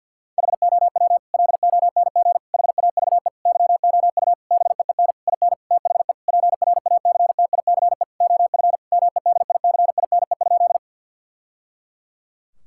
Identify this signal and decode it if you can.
Morse code, heard as a keyed tone, about 50 wpm, 700 Hz. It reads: HOW LONG HAVE YOU BEEN IN THE PRACTICE OF MEDICINE?
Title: Sentences from Top 1000 Words - Review Part 6 - 50wpm